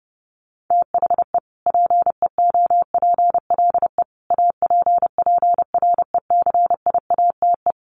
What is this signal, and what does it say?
Morse code, heard as a keyed tone, about 30 wpm, 700 Hz. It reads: THE PEOPLE APPRECIATE
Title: Sentences from Top 1300 Words 30wpm